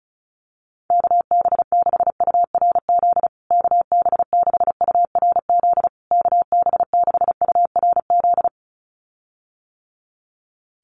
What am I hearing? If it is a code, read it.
Morse code, heard as a keyed tone, about 34 wpm, 700 Hz. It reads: KB6URZ KB6URZ KB6URZ
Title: USA Call Signs Repeated 3x 35wpm